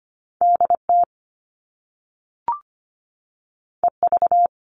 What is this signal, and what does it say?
Morse code, heard as a keyed tone, about 25 wpm, 700 Hz. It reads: DT EV
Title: All Cut Numbers - 2-Digit 25wpm